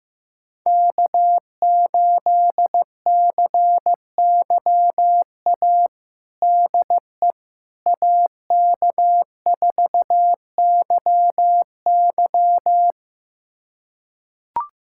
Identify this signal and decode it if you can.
Morse code, heard as a keyed tone, about 15 wpm, 700 Hz. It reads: K8CYA DE AK4YY
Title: USA Callsign DE Callsign - 5 Characters - 15wpm